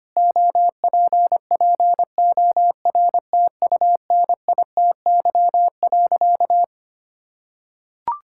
Morse code, heard as a keyed tone, about 25 wpm, 700 Hz. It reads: OPPORTUNITY.
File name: practice / 25wpm / Sentences with 4-Word Phrases - 25wpm